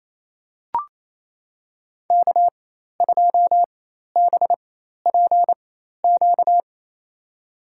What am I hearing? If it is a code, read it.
Morse code, heard as a keyed tone, about 28 wpm, spaced as for 13 wpm, 700 Hz. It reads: K2BPQ
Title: USA Call Signs - 5 Characters - 4x Character Spacing - 28wpm